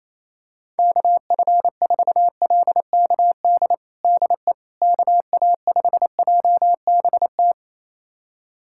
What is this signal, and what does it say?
Morse code, heard as a keyed tone, about 28 wpm, 700 Hz. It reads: KF4LKD DE KA5JBT
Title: USA Callsign DE Callsign - 6 Characters - 28wpm